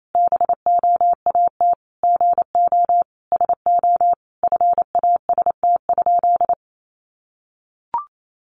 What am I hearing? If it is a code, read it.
Morse code, heard as a keyed tone, about 28 wpm, 700 Hz. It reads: BOAT GO SO FAST?
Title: Sentences from Top 500 Words 28wpm